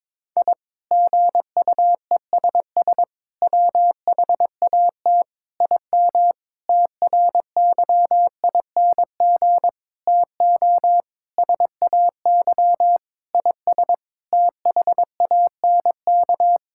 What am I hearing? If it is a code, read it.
Morse code, heard as a keyed tone, about 22 wpm, 700 Hz. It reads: I GUESS WHAT IM TRYING TO SAY IS THANK
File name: practice / 22wpm / Sentences with 8-Word Phrases - 22wpm